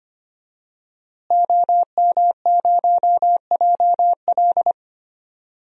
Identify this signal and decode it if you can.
Morse code, heard as a keyed tone, about 25 wpm, 700 Hz. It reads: OM0JL
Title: DXCC-100 Call Signs - 5 Characters - Repeated 3x - 25wpm